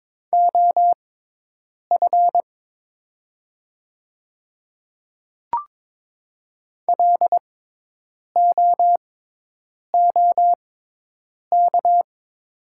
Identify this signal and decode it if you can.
Morse code, heard as a keyed tone, about 22 wpm, spaced as for 8 wpm, 700 Hz. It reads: OF LOOK
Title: 6x Character Spacing - Top 100 Words - 22wpm